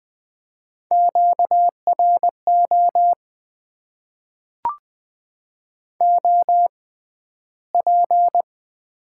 Morse code, heard as a keyed tone, about 20 wpm, spaced as for 12 wpm, 700 Hz. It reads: QRO OP
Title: Top 90 QSO Elements - 6x Character Spacing - 20wpm